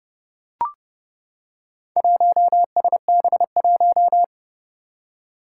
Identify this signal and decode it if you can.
Morse code, heard as a keyed tone, about 32 wpm, 700 Hz. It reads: 1SB1
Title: Four Challenging Letter-Number Combinations 30wpm